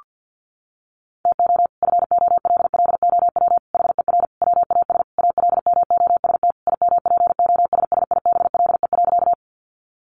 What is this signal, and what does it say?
Morse code, heard as a keyed tone, about 50 wpm, 700 Hz. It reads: TO FOLLOW HER WAS ALMOST IMPOSSIBLE.
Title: Sentences from Top 1100 Words 50wpm